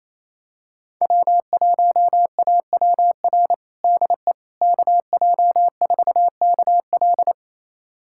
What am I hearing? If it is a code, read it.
Morse code, heard as a keyed tone, about 28 wpm, 700 Hz. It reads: W1AWR DE KJ4KL
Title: USA Callsign DE Callsign - 5 Characters - 28wpm